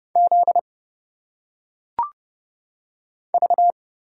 Morse code, heard as a keyed tone, about 31 wpm, 700 Hz. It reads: Z V
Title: Single Letters - Phonetic Pronunciation 30wpm